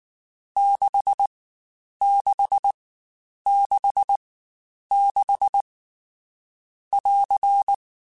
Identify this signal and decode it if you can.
Morse code, heard as a keyed tone, about 19 wpm, spaced as for 8 wpm, 800 Hz. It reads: 6666<AR>
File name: Short Runs / Short Runs-mp3s / Staccato@18